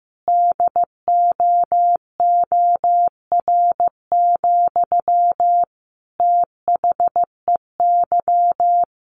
Morse code, heard as a keyed tone, about 15 wpm, 700 Hz. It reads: DOOR, THEY
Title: Sentences from Top 1400 Words 15wpm